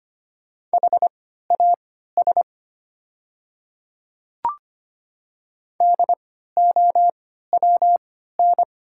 Morse code, heard as a keyed tone, about 25 wpm, spaced as for 14 wpm, 700 Hz. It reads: HAS DOWN
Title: 3x Character Spacing - Top 100 Words - 25wpm